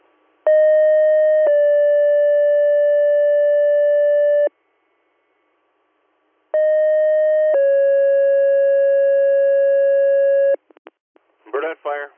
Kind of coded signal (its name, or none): two-tone page